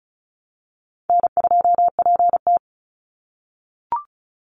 Morse code, heard as a keyed tone, about 36 wpm, 700 Hz. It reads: N2PT
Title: USA Call Signs 35wpm